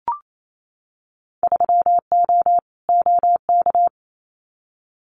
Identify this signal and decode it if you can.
Morse code, heard as a keyed tone, about 27 wpm, 700 Hz. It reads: 3O OK
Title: Winter Field Day Exchanges Encore 28wpm